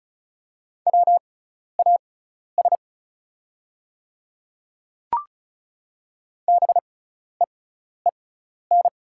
Morse code, heard as a keyed tone, about 34 wpm, spaced as for 12 wpm, 690 Hz. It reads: WAS BEEN
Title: 6x Character Spacing - Top 300 Words - 35wpm